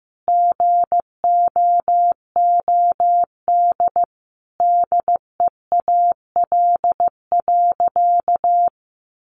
Morse code, heard as a keyed tone, about 15 wpm, 700 Hz. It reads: GOOD DEAL.
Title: Sentences from Top 1200 Words 15wpm